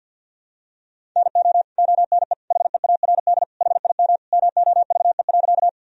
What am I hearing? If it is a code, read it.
Morse code, heard as a keyed tone, about 51 wpm, 700 Hz. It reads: NO ONE HEARD HIM MOVE.
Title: Sentences from Top 1000 Words - Review Part 1 - 50wpm